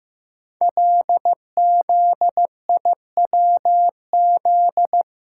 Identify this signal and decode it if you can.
Morse code, heard as a keyed tone, about 15 wpm, 700 Hz. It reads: LZIWZ
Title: Code Groups - 5 Letters - 15wpm